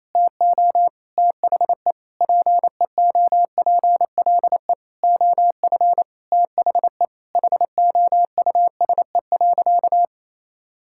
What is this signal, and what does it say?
Morse code, heard as a keyed tone, about 28 wpm, 700 Hz. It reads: TO THE PEOPLE OF THE HOUSE.
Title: Sentences from Top 1000 Words - Review Part 5 - 28wpm